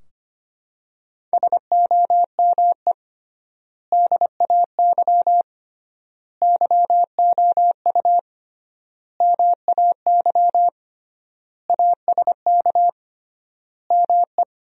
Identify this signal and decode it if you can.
Morse code, heard as a keyed tone, about 25 wpm, 700 Hz. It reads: SOME DAY YOU MAY ASK ME
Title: 3x Word-Spacing - Sentences from Top 200 Words - 25wpm